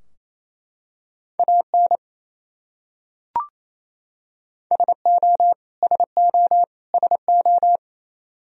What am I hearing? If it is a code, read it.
Morse code, heard as a keyed tone, about 28 wpm, 700 Hz. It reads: AN SO SO SO
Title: Top 100 Words Repeated 28wpm